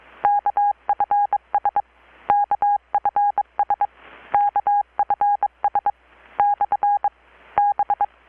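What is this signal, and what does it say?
Morse code, heard as a keyed tone, about 22 wpm, spaced as for 18 wpm, 820 Hz. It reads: KFS KFS KFS/B
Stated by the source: Morse from KPH's Night of Nights XX maritime radiotelegraphy event, received in Chiapas, Mexico about 3,800 km away on a Kenwood R-600 with a grounded 10 m long-wire antenna.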